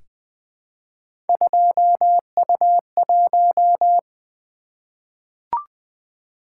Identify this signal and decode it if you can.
Morse code, heard as a keyed tone, about 21 wpm, 700 Hz. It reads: 2U1
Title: Three Letter-Number Combinations - 20wpm